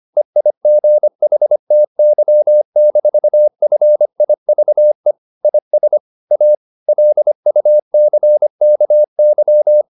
Morse code, heard as a keyed tone, about 25 wpm, 600 Hz. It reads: EIGHTY-FIVE IS A LUCKY
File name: the old man and the sea part 1 25wpm